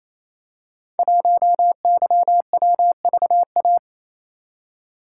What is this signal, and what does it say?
Morse code, heard as a keyed tone, about 29 wpm, 700 Hz. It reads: 1YWVA
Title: Code Groups - 5 Letter:Numbers - 28wpm